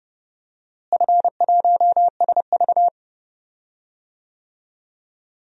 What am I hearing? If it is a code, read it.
Morse code, heard as a keyed tone, about 31 wpm, 700 Hz. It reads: F1SV
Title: DXCC-100 Call Signs - 4 Characters - Repeated 2x - 30wpm